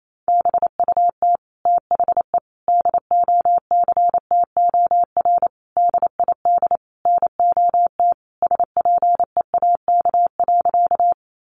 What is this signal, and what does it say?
Morse code, heard as a keyed tone, about 28 wpm, 700 Hz. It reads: BUT THE DOCTOR DID NOT SPEAK.
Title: Sentences from Top 1500 Words 28wpm